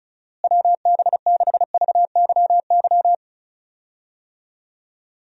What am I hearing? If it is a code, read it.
Morse code, heard as a keyed tone, about 35 wpm, 700 Hz. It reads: WB6VYY
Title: USA Call Signs - 6 Characters - Repeated 2x - 35wpm